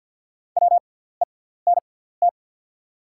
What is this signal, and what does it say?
Morse code, heard as a keyed tone, about 48 wpm, spaced as for 17 wpm, 700 Hz. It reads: WENT
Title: 6x Character Spacing - Top 200 Words - 50wpm